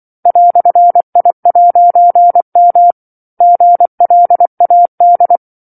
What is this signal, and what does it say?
Morse code, heard as a keyed tone, about 24 wpm, 700 Hz. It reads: "I'M GLAD
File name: 07-Chapter-07 24 WPM